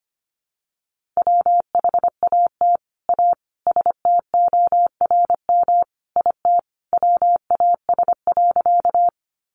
Morse code, heard as a keyed tone, about 25 wpm, 700 Hz. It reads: WHAT A STORM IT WAS.
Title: Sentences from Top 1500 Words 25wpm